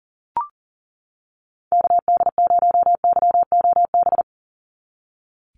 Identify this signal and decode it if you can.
Morse code, heard as a keyed tone, about 38 wpm, 700 Hz. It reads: KD0YOB
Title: NA SOTA Activator Callsigns 40wpm